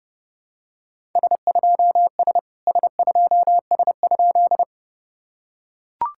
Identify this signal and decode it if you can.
Morse code, heard as a keyed tone, about 30 wpm, 700 Hz. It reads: S2S S2S?
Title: SOTA QSO Practice 30wpm